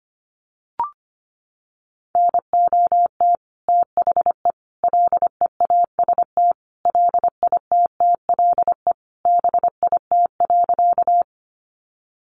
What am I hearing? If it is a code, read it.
Morse code, heard as a keyed tone, about 25 wpm, 700 Hz. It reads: NOT THE LEAST LITTLE BIT.
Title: Sentences from Top 900 Words 25wpm